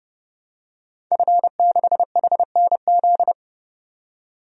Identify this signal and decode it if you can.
Morse code, heard as a keyed tone, about 32 wpm, 700 Hz. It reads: F6HNZ
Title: DXCC-100 Call Signs - 5 Characters - 30wpm